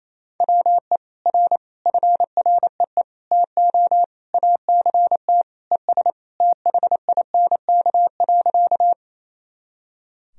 Morse code, heard as a keyed tone, about 28 wpm, 700 Hz. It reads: WE R FREE TO ACT ES THINK.
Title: Sentences from Top 500 Words and Abbreviations 28wpm